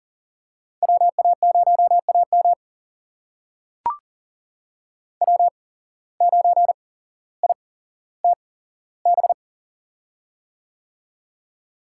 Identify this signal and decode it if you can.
Morse code, heard as a keyed tone, about 40 wpm, spaced as for 17 wpm, 700 Hz. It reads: WA0AM W9ITB
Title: USA Call Signs - 5 Characters - 8x Character Spacing - 40wpm